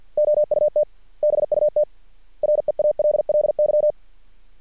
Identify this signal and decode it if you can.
Morse code, heard as a keyed tone, about 50 wpm, 600 Hz. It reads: OUT BUT REALLY
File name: archived / CRSnet